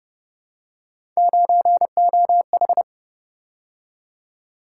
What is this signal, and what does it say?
Morse code, heard as a keyed tone, about 29 wpm, 700 Hz. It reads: 9OH